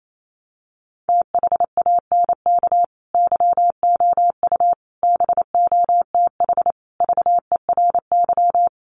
Morse code, heard as a keyed tone, about 28 wpm, 700 Hz. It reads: THANK YOU BOTH VERY
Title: Sets of 5 Words 28wpm